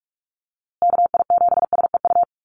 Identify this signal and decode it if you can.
Morse code, heard as a keyed tone, about 44 wpm, 700 Hz. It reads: KI7SEU
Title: USA Call Signs Repeated 3x 45wpm